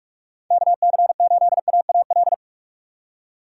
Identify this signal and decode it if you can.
Morse code, heard as a keyed tone, about 46 wpm, 700 Hz. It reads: KC8AAL
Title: USA Call Signs - 4x Character Spacing - 45wpm